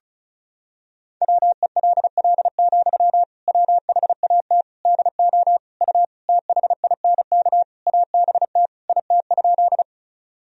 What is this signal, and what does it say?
Morse code, heard as a keyed tone, about 36 wpm, 700 Hz. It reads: WELL, WHAT DO U THINK ABT IT?
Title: Sentences from Top 200 Words and Abbreviations 35wpm